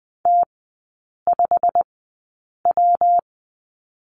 Morse code, heard as a keyed tone, about 20 wpm, spaced as for 8 wpm, 700 Hz. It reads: T5W
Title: 127 Lesson 21e - Number 6 - Three Characters 20wpm